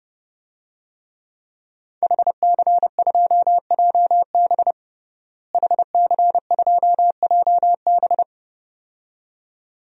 Morse code, heard as a keyed tone, about 32 wpm, 700 Hz. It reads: HC2JB HC2JB
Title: DXCC-100 Call Signs - 5 Characters - Repeated 2x - 30wpm